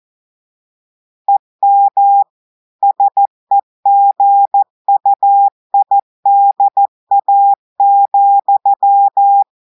Morse code, heard as a keyed tone, about 14 wpm, 800 Hz. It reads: EM SEGUIDA,
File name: audios / portugais / portugais texte en clair 2 vitesse 14 mots minutes